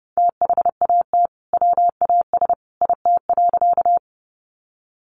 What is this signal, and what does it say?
Morse code, heard as a keyed tone, about 30 wpm, 700 Hz. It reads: THAT WAS IT.
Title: Sentences from Top 100 Words 30wpm